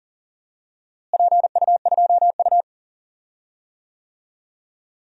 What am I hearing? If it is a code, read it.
Morse code, heard as a keyed tone, about 40 wpm, 700 Hz. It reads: PU2U
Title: DXCC-100 Call Signs - 4 Characters - Repeated 3x - 40wpm